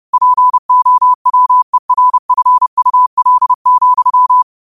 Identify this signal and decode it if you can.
Morse code, heard as a keyed tone, about 30 wpm, 1000 Hz. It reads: POWERFUL,